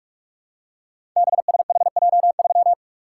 Morse code, heard as a keyed tone, about 44 wpm, 700 Hz. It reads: BSHJ3
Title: Code Groups - 5 Letter:Numbers - 45wpm